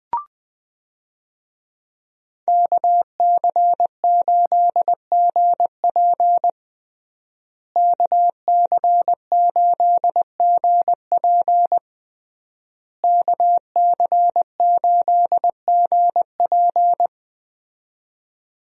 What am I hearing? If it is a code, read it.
Morse code, heard as a keyed tone, about 21 wpm, 700 Hz. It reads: KC8GP KC8GP KC8GP
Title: USA Call Signs - 5 Characters - Repeated 3x - 20wpm